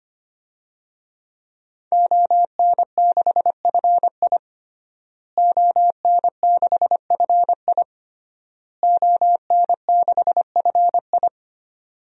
Morse code, heard as a keyed tone, about 25 wpm, 700 Hz. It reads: ON6FI ON6FI ON6FI
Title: DXCC-100 Call Signs - 5 Characters - Repeated 3x - 25wpm